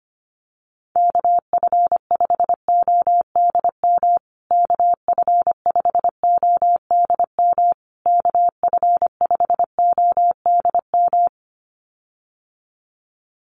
Morse code, heard as a keyed tone, about 25 wpm, 700 Hz. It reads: KF5ODM KF5ODM KF5ODM
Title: USA Call Signs Repeated 3x 25wpm